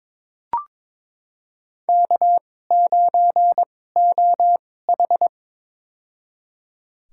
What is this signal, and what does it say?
Morse code, heard as a keyed tone, about 22 wpm, spaced as for 16 wpm, 700 Hz. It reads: K9OH
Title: USA Call Signs - 2x Character Spacing - 22wpm